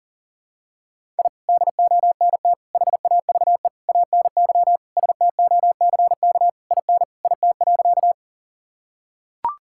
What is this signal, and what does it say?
Morse code, heard as a keyed tone, about 40 wpm, 700 Hz. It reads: I DONT HAVE ANY STOCK IN IT.
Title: Sentences from Top 1000 Words - Review Part 5 - 40wpm